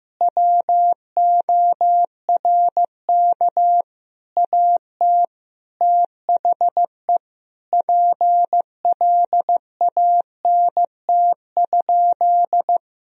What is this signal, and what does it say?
Morse code, heard as a keyed tone, about 15 wpm, 700 Hz. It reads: WORK AT THE PLANT?
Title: Sentences from Top 1000 Words - Review Part 1 - 15wpm